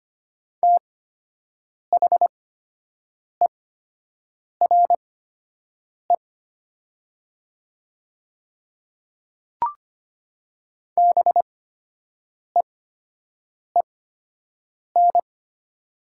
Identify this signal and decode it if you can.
Morse code, heard as a keyed tone, about 25 wpm, spaced as for 7 wpm, 700 Hz. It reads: THERE BEEN
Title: 8x Character Spacing - Top 100 Words - 25wpm